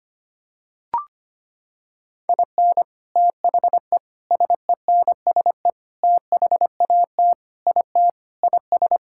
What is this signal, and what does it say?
Morse code, heard as a keyed tone, about 25 wpm, 700 Hz. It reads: IN THE SENSE THAT IT IS
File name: practice / 25wpm / Sentences with 4-Word Phrases - 25wpm